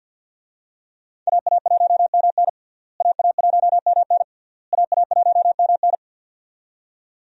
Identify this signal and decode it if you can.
Morse code, heard as a keyed tone, about 52 wpm, 700 Hz. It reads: AA1MN AA1MN AA1MN
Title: USA Call Signs - 5 Characters - Repeated 3x - 50wpm